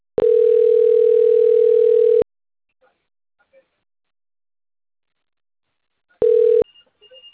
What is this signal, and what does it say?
Ringback tone (North America), 440 + 480 Hz.